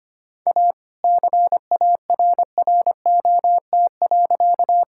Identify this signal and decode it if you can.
Morse code, heard as a keyed tone, about 25 wpm, 700 Hz. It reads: A CARROT.
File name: practice / 25wpm / All-Age Appropriate Jokes - 25wpm